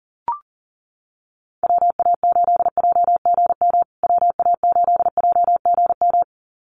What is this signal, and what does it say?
Morse code, heard as a keyed tone, about 40 wpm, 700 Hz. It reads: WA8JGM WA8JGM
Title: USA Call Signs Repeated 2x 40wpm